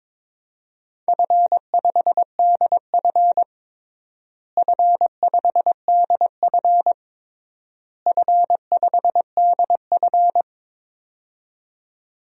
Morse code, heard as a keyed tone, about 21 wpm, 700 Hz. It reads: F5DF F5DF F5DF
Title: DXCC-100 Call Signs - 4 Characters - Repeated 3x - 22wpm